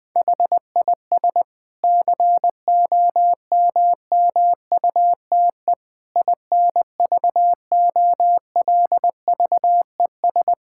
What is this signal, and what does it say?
Morse code, heard as a keyed tone, about 20 wpm, 700 Hz. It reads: HIS COMMUTE INVOLVES
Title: Difficult Sentences with Hints - 20wpm